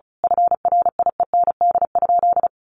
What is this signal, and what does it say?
Morse code, heard as a keyed tone, about 34 wpm, 700 Hz. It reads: FRIEND?
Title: Sentences from Top 300 Words 35wpm